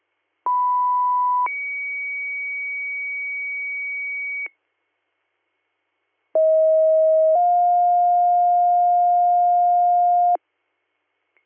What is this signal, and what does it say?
Two-tone page, heard as a one-second tone, then a three-second tone, about 990 then 2250 Hz, about 640 then 710 Hz. It is heard from 0.5 to 10.4 s.